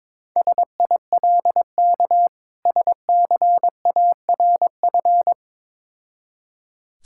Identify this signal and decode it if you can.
Morse code, heard as a keyed tone, about 22 wpm, 690 Hz. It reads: SILK SCARF
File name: Difficult Words with Hints - 22wpm